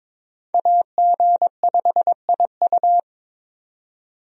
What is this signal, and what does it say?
Morse code, heard as a keyed tone, about 22 wpm, 700 Hz. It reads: AG5IU
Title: USA Call Signs - 5 Characters - 22wpm